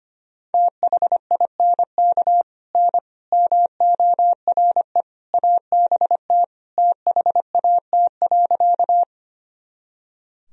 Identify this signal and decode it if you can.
Morse code, heard as a keyed tone, about 25 wpm, 700 Hz. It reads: THINK N MORE ABT THAT.
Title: Sentences from Top 200 Words and Abbreviations 25wpm